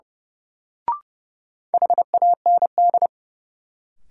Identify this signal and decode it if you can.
Morse code, heard as a keyed tone, about 30 wpm, 700 Hz. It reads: HAND
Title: Sentences from Top 200 Words - New Word Drill - 30wpm